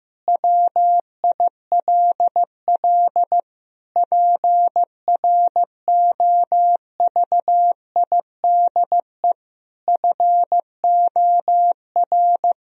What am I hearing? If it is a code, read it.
Morse code, heard as a keyed tone, about 15 wpm, 700 Hz. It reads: WILL PROVIDE FOR
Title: Sentences from Top 1000 Words - Review Part 3 - 15wpm